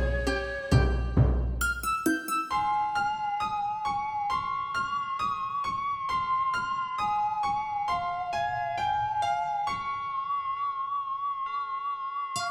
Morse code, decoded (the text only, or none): none